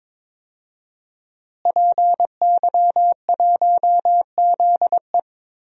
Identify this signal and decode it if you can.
Morse code, heard as a keyed tone, about 22 wpm, 700 Hz. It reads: PY1ZE
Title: DXCC-100 Call Signs - 5 Characters - Repeated 3x - 22wpm